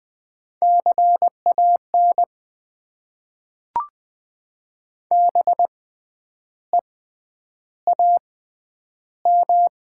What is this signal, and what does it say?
Morse code, heard as a keyed tone, about 20 wpm, spaced as for 9 wpm, 700 Hz. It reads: CAN BEAM
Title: Top 90 QSO Elements - 6x Character Spacing - 20wpm